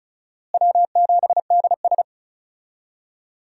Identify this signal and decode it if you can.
Morse code, heard as a keyed tone, about 36 wpm, 700 Hz. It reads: W7DS
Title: USA Call Signs - 4 Characters - Repeated 3x - 35wpm